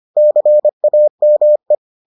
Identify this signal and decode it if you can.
Morse code, heard as a keyed tone, about 25 wpm, 600 Hz. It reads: CAME